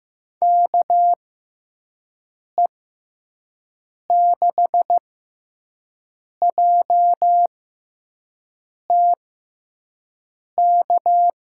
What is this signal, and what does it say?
Morse code, heard as a keyed tone, about 15 wpm, spaced as for 5 wpm, 700 Hz. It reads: KE6JTK
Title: USA Call Signs - 6 Characters - 6x Character Spacing - 15wpm